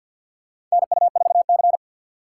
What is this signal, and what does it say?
Morse code, heard as a keyed tone, about 46 wpm, 700 Hz. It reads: NU4C
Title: USA Call Signs - 4x Character Spacing - 50wpm